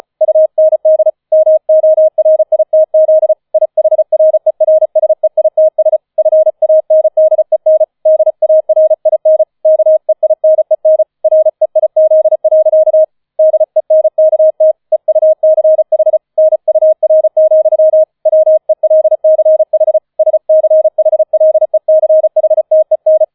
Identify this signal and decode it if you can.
Morse code, heard as a keyed tone, about 34 wpm, 630 Hz. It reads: UND MORITZ IHRERSEITS FANDEN DARIN KEINEN REIZ. DENKT EUCH NUR, WELCH SCHLECHTEN
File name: 005 Fünfter Streich 35wpm